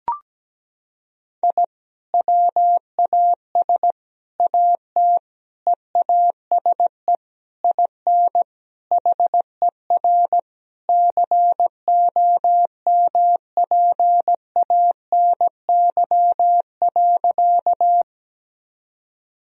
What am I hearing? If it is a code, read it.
Morse code, heard as a keyed tone, about 17 wpm, 700 Hz. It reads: I WAS AT EASE IN HER COMPANY.
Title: Sentences from Top 1000 Words - Review Part 1 - 17wpm